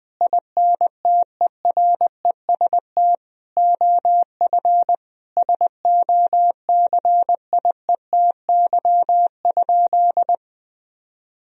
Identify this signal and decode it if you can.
Morse code, heard as a keyed tone, about 21 wpm, 700 Hz. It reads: INTEREST OF SOCIETY?
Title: Sentences from Top 1000 Words - Review Part 4 - 20wpm